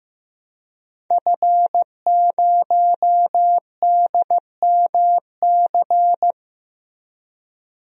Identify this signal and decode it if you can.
Morse code, heard as a keyed tone, about 15 wpm, 700 Hz. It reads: F0DMC
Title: DXCC-100 Call Signs - 5 Characters - 15wpm